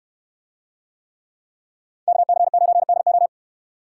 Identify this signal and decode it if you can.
Morse code, heard as a keyed tone, about 64 wpm, 700 Hz. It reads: KC9DO